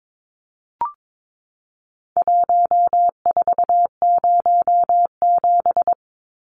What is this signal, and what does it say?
Morse code, heard as a keyed tone, about 22 wpm, 700 Hz. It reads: 1407
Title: Numbers - Contest Style - 4-Digit 22wpm